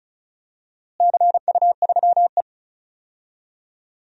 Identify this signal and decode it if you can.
Morse code, heard as a keyed tone, about 34 wpm, 700 Hz. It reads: CU3E
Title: DXCC-100 Call Signs - 4 Characters - Repeated 3x - 35wpm